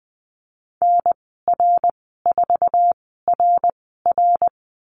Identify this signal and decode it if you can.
Morse code, heard as a keyed tone, about 20 wpm, spaced as for 14 wpm, 700 Hz. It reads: NR4RR
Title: 062 Lesson 11g - Letter h - Call Signs 20wpm